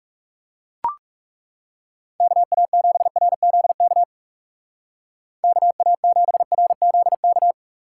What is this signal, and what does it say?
Morse code, heard as a keyed tone, about 36 wpm, 700 Hz. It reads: KA7RZK KA7RZK
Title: Speed Racing - USA 2x3 Call Signs 30wpm